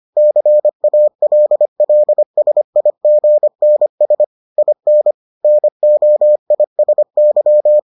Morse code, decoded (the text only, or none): CALLSIGNS IN NOISY